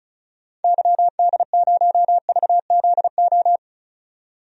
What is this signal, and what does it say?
Morse code, heard as a keyed tone, about 34 wpm, 700 Hz. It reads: YD0VZO
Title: DXCC-100 Call Signs - 6 Characters - 35wpm